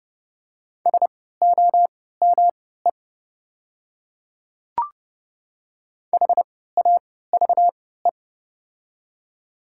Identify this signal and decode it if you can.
Morse code, heard as a keyed tone, about 30 wpm, spaced as for 17 wpm, 690 Hz. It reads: SOME HAVE
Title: 3x Character Spacing - Top 200 Words - 30wpm